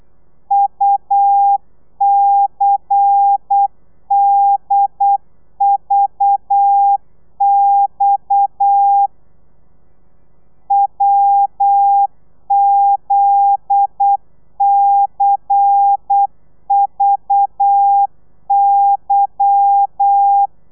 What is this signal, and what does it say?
Morse code, heard as a keyed tone, about 8 wpm, 800 Hz. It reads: UCDVX WZCVY
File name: telegrafia / i2tfj / Lez10 35